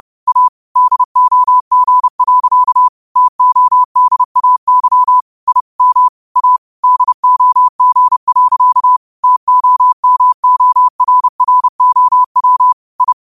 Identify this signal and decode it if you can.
Morse code, heard as a keyed tone, about 30 wpm, 1000 Hz. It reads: A DOG. TODAY I M A DOG. TOMORROW I
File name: qotdfiles / 30wpm